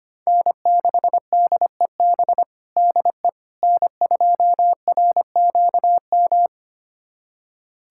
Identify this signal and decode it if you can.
Morse code, heard as a keyed tone, about 25 wpm, 700 Hz. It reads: N6DEB DE N2RQM
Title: USA Callsign DE Callsign - 5 Characters - 25wpm